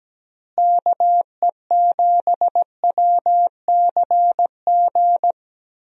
Morse code, heard as a keyed tone, about 17 wpm, 700 Hz. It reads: KE7WCG
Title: USA Call Signs - 6 Characters - Repeated 3x - 17wpm